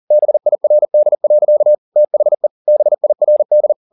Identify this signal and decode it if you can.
Morse code, heard as a keyed tone, about 40 wpm, 600 Hz. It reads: BIRD. THE BIRD